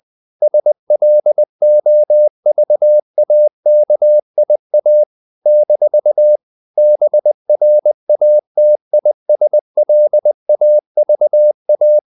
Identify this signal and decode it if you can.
Morse code, heard as a keyed tone, about 20 wpm, 600 Hz. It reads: SLOVAKIA - BRATISLAVA
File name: CW / Practice / NationalCapitals20